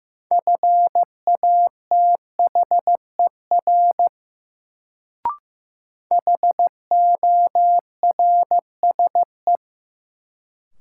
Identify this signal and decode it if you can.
Morse code, heard as a keyed tone, about 15 wpm, 700 Hz. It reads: FATHER HORSE
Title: Sentences from Top 300 Words - New Word Drill - 15wpm